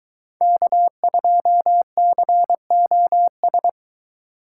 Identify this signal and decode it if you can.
Morse code, heard as a keyed tone, about 22 wpm, 700 Hz. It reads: K2COS